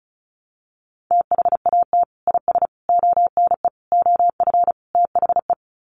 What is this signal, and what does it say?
Morse code, heard as a keyed tone, about 34 wpm, 700 Hz. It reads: THAT IS ONE OF THE
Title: Sets of 5 Words Encore 35wpm